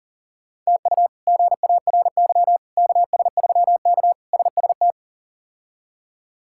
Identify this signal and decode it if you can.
Morse code, heard as a keyed tone, about 40 wpm, 700 Hz. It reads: TU GARY KS3K SST
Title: K1USN SST QSO Practice - 40wpm